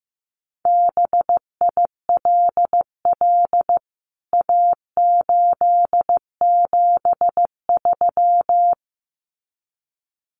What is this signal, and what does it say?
Morse code, heard as a keyed tone, about 15 wpm, 700 Hz. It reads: BILL A873
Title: CWT Exchange 15wpm